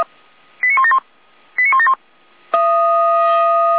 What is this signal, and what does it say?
Five-tone selective call (ZVEI), heard as quick runs of five tones, tones of about 70 ms. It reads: 89171 89171
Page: /455/SDR/ZVEI/89167/